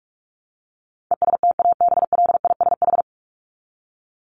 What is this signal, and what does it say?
Morse code, heard as a keyed tone, about 44 wpm, 700 Hz. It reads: ESTABLISH
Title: ED Suffix - Top 100 Words with Base Words 45wpm